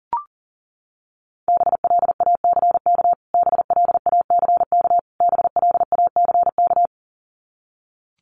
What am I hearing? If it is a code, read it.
Morse code, heard as a keyed tone, about 40 wpm, 700 Hz. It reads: BLACK BLACK BLACK
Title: Top 500 Words Repeated 40wpm